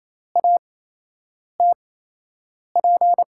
Morse code, heard as a keyed tone, about 28 wpm, spaced as for 8 wpm, 700 Hz. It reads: ATP
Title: USA Call Signs - 6 Characters - 8x Character Spacing - 28wpm